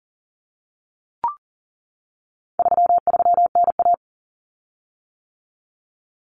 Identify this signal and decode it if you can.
Morse code, heard as a keyed tone, about 40 wpm, 700 Hz. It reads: 33NA
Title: Common Cut Numbers - 4-Digit 40wpm